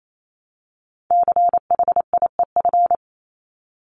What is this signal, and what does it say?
Morse code, heard as a keyed tone, about 28 wpm, 700 Hz. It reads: CHIEF
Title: LY Suffix - Top 100 Words with Base Words 28wpm